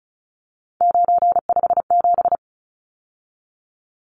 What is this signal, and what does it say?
Morse code, heard as a keyed tone, about 34 wpm, 700 Hz. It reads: 957
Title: Numbers - 3-Digit 35wpm